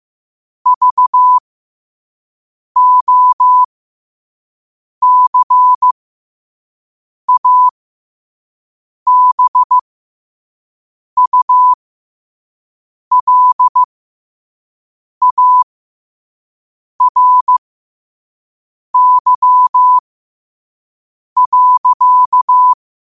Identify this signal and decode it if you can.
Morse code, heard as a keyed tone, about 15 wpm, spaced as for 5 wpm, 1000 Hz. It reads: VOCABULARY.
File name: qotdfiles / 5wpm